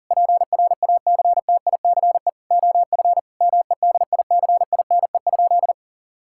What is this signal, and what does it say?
Morse code, heard as a keyed tone, about 40 wpm, 700 Hz. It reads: PRACTICE OF MEDICINE?
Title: Sentences from Top 1000 Words - Review Part 6 - 40wpm